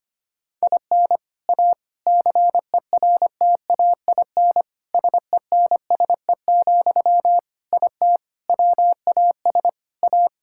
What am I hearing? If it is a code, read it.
Morse code, heard as a keyed tone, about 25 wpm, 700 Hz. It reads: IN A CERTAIN SENSE, IT WAS A
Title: Sentences from Top 1000 Words - Review Part 6 - 25wpm